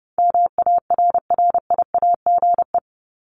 Morse code, heard as a keyed tone, about 30 wpm, 700 Hz. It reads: MARRIAGE
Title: Top 751-1000 Words 30wpm